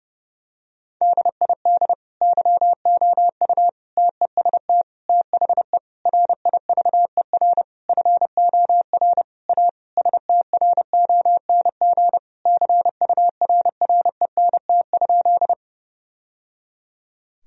Morse code, heard as a keyed tone, about 31 wpm, 700 Hz. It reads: DID YOU TEST THE RIVER FOR A STRONG CURRENT?
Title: Sentences from Top 1000 Words - Review Part 4 - 30wpm